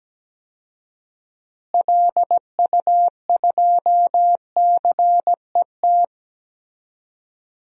Morse code, heard as a keyed tone, about 17 wpm, 700 Hz. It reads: LU2CET
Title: DXCC-100 Call Signs - 6 Characters - Repeated 2x - 17wpm